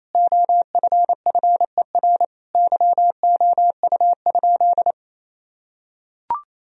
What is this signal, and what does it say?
Morse code, heard as a keyed tone, about 28 wpm, 700 Hz. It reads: OFFER YOU?
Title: Sentences from Top 1000 Words - Review Part 4 - 28wpm